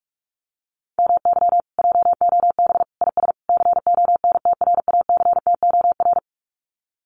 Morse code, heard as a keyed tone, about 46 wpm, 700 Hz. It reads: MY JOB IS CONTRACTOR
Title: Common QSO Phrases Encore 45wpm